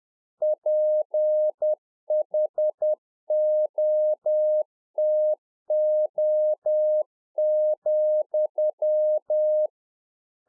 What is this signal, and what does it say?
Morse code, heard as a keyed tone, about 10 wpm, 610 Hz. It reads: PHOTO,